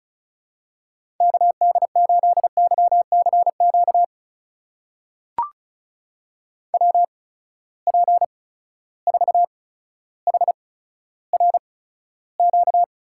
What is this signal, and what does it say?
Morse code, heard as a keyed tone, about 36 wpm, spaced as for 15 wpm, 700 Hz. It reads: KD8YCQ WP4HRQ
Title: USA Call Signs - 6 Characters - 8x Character Spacing - 35wpm